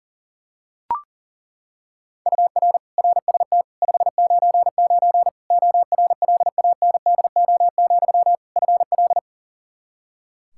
Morse code, heard as a keyed tone, about 40 wpm, 700 Hz. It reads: UR RST 599 ORLANDO, FL
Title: Practice QSO - Exchange 1 - Beginning - 40wpm